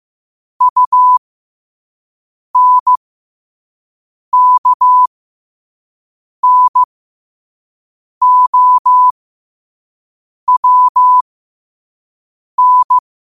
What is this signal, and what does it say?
Morse code, heard as a keyed tone, about 15 wpm, spaced as for 5 wpm, 1000 Hz. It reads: UNKNOWN
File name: qotdfiles / 5wpm